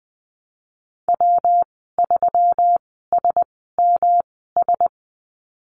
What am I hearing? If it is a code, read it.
Morse code, heard as a keyed tone, about 20 wpm, spaced as for 15 wpm, 700 Hz. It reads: W3SMS
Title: 161 Lesson 26g - Letter g - Call Signs 20wpm